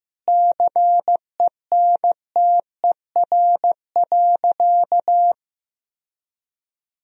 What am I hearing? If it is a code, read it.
Morse code, heard as a keyed tone, about 14 wpm, 700 Hz. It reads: CENTER.
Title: Sentences from Top 1000 Words - Review Part 2 - 15wpm